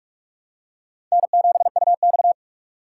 Morse code, heard as a keyed tone, about 46 wpm, 700 Hz. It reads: N7UX
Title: USA Call Signs - 4 Characters - 4x Character Spacing - 45wpm